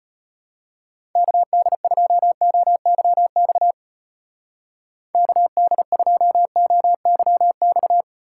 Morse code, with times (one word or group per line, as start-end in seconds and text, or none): 1.2-3.7 KD2OYX
5.1-8.0 KD2OYX